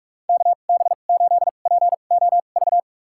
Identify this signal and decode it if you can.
Morse code, heard as a keyed tone, about 46 wpm, spaced as for 33 wpm, 700 Hz. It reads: KB8POV
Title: USA Call Signs - 6 Characters - 2x Character Spacing - 45wpm